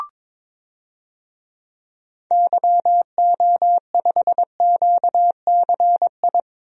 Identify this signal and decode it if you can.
Morse code, heard as a keyed tone, about 22 wpm, 700 Hz. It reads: YO5QCI
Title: DXCC-100 Call Signs - 6 Characters - 22wpm